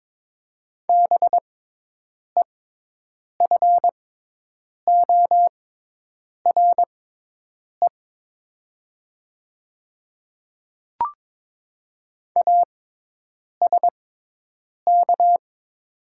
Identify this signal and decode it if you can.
Morse code, heard as a keyed tone, about 22 wpm, spaced as for 8 wpm, 700 Hz. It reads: BEFORE ASK
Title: 6x Character Spacing - Top 300 Words - 22wpm